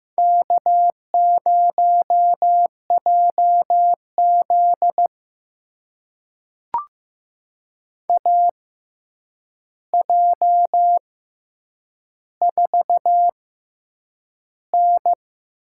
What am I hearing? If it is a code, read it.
Morse code, heard as a keyed tone, about 15 wpm, spaced as for 8 wpm, 700 Hz. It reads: K0JZ AJ4N
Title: USA Call Signs - 4 Characters - 6x Character Spacing - 15wpm